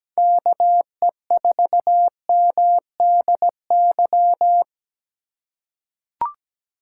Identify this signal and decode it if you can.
Morse code, heard as a keyed tone, about 17 wpm, 700 Hz. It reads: KE4MDY
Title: USA Call Signs - 6 Characters - 8x Character Spacing - 17wpm